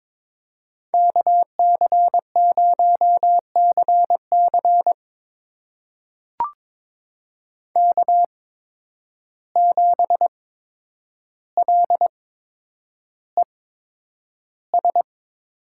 Morse code, heard as a keyed tone, about 22 wpm, spaced as for 9 wpm, 700 Hz. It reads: KC0CC K7LES
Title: USA Call Signs - 5 Characters - 8x Character Spacing - 22wpm